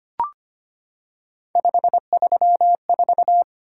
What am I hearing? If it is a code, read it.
Morse code, heard as a keyed tone, about 25 wpm, 700 Hz. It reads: 534
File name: Three Letter-Number Combinations - 25wpm